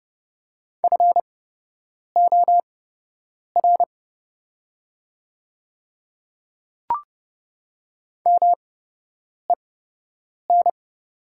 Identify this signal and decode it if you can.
Morse code, heard as a keyed tone, about 30 wpm, spaced as for 8 wpm, 700 Hz. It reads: FOR MEN
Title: 8x Character Spacing - Top 200 Words - 30wpm